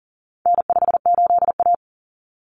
Morse code, heard as a keyed tone, about 40 wpm, 700 Hz. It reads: N58A
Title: Common Cut Numbers - 4-Digit 40wpm